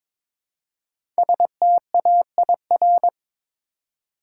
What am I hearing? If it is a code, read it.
Morse code, heard as a keyed tone, about 22 wpm, 700 Hz. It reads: STAIR
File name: 5 Letter Words - First 3 Letters Given - 22wpm